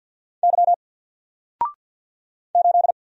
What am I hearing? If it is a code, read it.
Morse code, heard as a keyed tone, about 48 wpm, 700 Hz. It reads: Y 8
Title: Single Letter-Number Alt 50wpm